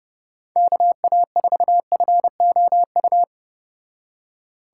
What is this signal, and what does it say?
Morse code, heard as a keyed tone, about 29 wpm, 700 Hz. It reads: KA4FOU